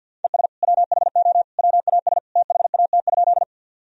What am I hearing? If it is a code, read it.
Morse code, heard as a keyed tone, about 50 wpm, 700 Hz. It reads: ES WHO WAS THAT?
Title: Sentences from Top 100 Words and Abbreviations 50wpm